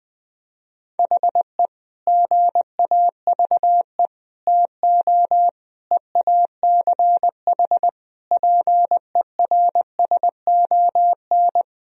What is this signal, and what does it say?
Morse code, heard as a keyed tone, about 21 wpm, 700 Hz. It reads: HE GAVE TO EACH PERSON
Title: Sentences from Top 1000 Words - Review Part 4 - 20wpm